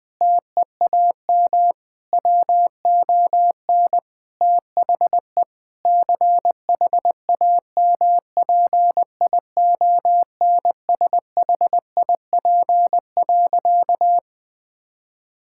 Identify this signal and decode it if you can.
Morse code, heard as a keyed tone, about 20 wpm, 700 Hz. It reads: TEAM WON THE CHAMPIONSHIP.
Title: Sentences with 8-Word Phrases - 20wpm